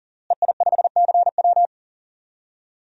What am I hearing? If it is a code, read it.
Morse code, heard as a keyed tone, about 40 wpm, 700 Hz. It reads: EI5CW
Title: DXCC-100 Call Signs - 5 Characters - Repeated 2x - 40wpm